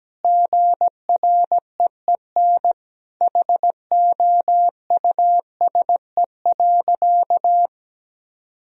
Morse code, heard as a keyed tone, about 17 wpm, 700 Hz. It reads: GREEN HOUSE.
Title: Sentences from Top 1000 Words - Review Part 2 - 17wpm